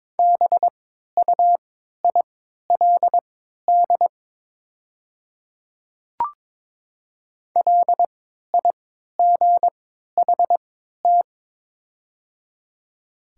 Morse code, heard as a keyed tone, about 22 wpm, spaced as for 13 wpm, 700 Hz. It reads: BUILD LIGHT
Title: 3x Character Spacing - Top 200 Words - 22wpm